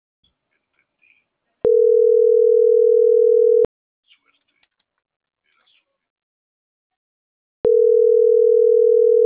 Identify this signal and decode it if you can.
Ringback tone (North America), 440 + 480 Hz.
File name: CENTRAL SAGOT / out xquenda.sagot